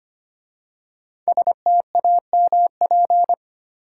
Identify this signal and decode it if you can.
Morse code, heard as a keyed tone, about 25 wpm, 700 Hz. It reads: STAMP
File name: 5 Letter Words - First 3 Letters Given - 25wpm